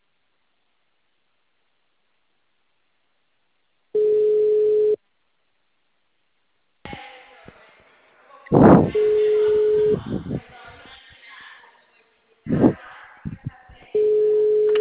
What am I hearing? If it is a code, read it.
Ringback tone (Europe, 425 Hz), with speech.
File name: out- -2044- - - .85256